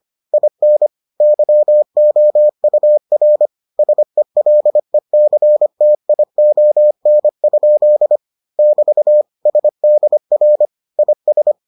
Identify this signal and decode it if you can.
Morse code, heard as a keyed tone, about 25 wpm, 600 Hz. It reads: IN YOUR SELECTION? <BT> SDR IS